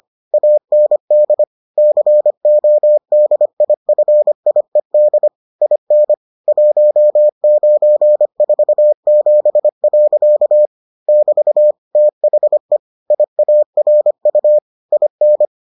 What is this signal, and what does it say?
Morse code, heard as a keyed tone, about 25 wpm, 600 Hz. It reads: AND CODIFIED IN 1947. <BT> THE IARU IN